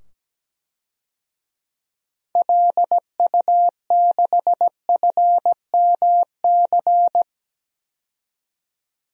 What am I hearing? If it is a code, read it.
Morse code, heard as a keyed tone, about 17 wpm, 700 Hz. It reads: LU6FMC
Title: DXCC-100 Call Signs - 6 Characters - 17wpm